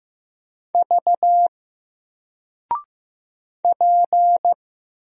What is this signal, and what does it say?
Morse code, heard as a keyed tone, about 15 wpm, 700 Hz. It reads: V P
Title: Single Letter-Number 15wpm